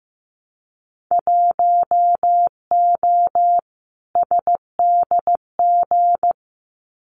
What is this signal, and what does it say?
Morse code, heard as a keyed tone, about 15 wpm, 700 Hz. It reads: 1O SDG
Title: Winter Field Day Exchanges Encore 15wpm